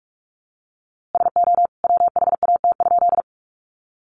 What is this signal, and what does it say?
Morse code, heard as a keyed tone, about 44 wpm, 700 Hz. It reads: SO WHAT?
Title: A Sound Of Thunder 45wpm